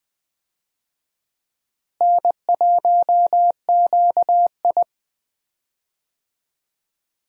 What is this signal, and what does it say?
Morse code, heard as a keyed tone, about 21 wpm, 700 Hz. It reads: N1QI
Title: USA Call Signs - 4 Characters - Repeated 3x - 20wpm